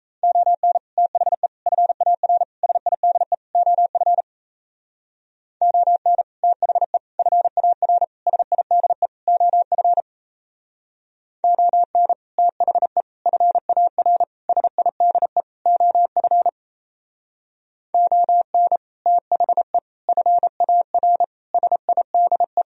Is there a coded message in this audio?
Yes: Morse code from 0.2 to 22.6 s, heard as a keyed tone, about 36 wpm, 700 Hz. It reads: ON THE FAR SIDE OF ON THE FAR SIDE OF ON THE FAR SIDE OF ON THE FAR SIDE